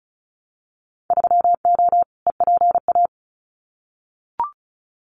Morse code, heard as a keyed tone, about 36 wpm, 700 Hz. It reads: 3O EPA
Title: Winter Field Day Exchanges 35wpm